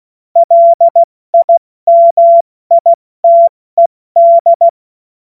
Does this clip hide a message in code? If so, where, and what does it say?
Yes: Morse code from 0.4 to 4.7 s, heard as a keyed tone, about 16 wpm, spaced as for 14 wpm, 680 Hz. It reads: LIMITED